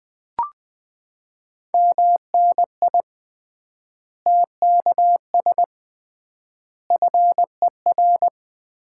Morse code, heard as a keyed tone, about 20 wpm, 700 Hz. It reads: MNI TKS FER
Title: Common QSO Phrases - 3x Word-Spacing - 20wpm